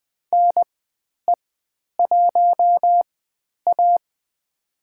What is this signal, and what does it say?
Morse code, heard as a keyed tone, about 20 wpm, spaced as for 10 wpm, 700 Hz. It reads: NE1A
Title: 095 Lesson 16g - Letter c - Call Signs 20wpm